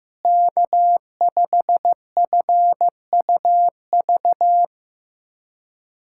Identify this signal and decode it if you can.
Morse code, heard as a keyed tone, about 16 wpm, 700 Hz. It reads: K5FUV
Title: K1USN SST QSO Practice - 15wpm